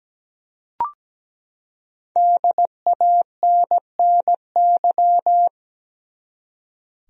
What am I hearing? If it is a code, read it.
Morse code, heard as a keyed tone, about 17 wpm, 700 Hz. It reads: DANNY
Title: USA Common Names - Max 5 Letters - 17wpm